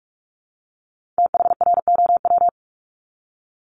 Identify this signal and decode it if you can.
Morse code, heard as a keyed tone, about 46 wpm, 700 Hz. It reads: THROW
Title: Multi-letter Phonemes 45wpm